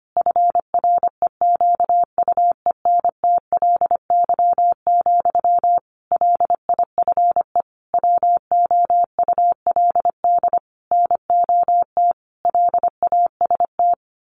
Morse code, heard as a keyed tone, about 25 wpm, 700 Hz. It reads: FREQUENTLY, LIFE WOULD NOT LAST
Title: Sentences from Top 1300 Words 25wpm